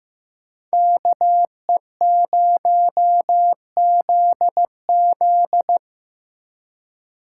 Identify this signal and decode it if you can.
Morse code, heard as a keyed tone, about 18 wpm, 700 Hz. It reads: KE0ZZ
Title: Speed Racing - USA 2x2 Call Signs 15wpm